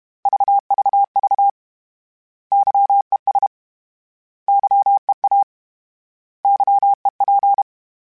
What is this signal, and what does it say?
Morse code, heard as a keyed tone, about 32 wpm, 800 Hz. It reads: VVV YES YEA YEP
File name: Misc / Misc-mp3s / Misc @30